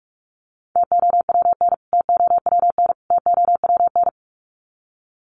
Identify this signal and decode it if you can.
Morse code, heard as a keyed tone, about 44 wpm, 700 Hz. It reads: TOWN TOWN TOWN
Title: Top 500 Words Repeated 45wpm